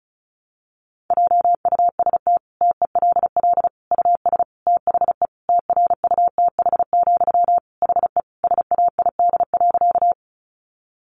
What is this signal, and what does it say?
Morse code, heard as a keyed tone, about 34 wpm, 700 Hz. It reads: JUST TELL US THE TRUTH, HE SAID.
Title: Sentences from Top 800 Words 35wpm